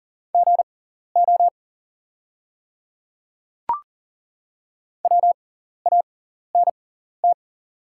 Morse code, heard as a keyed tone, about 40 wpm, spaced as for 14 wpm, 700 Hz. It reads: GO WANT
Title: 6x Character Spacing - Top 200 Words - 40wpm